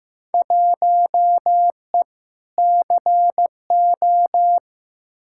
Morse code, heard as a keyed tone, about 15 wpm, 700 Hz. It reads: 1E CO
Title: Field Day Exchanges - 15wpm